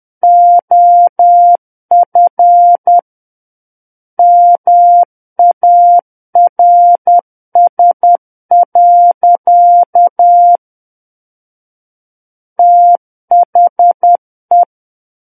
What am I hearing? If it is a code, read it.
Morse code, heard as a keyed tone, about 10 wpm, 700 Hz. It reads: OF MARS. THE
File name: Book One / 01-Chapter 1 - 10 WPM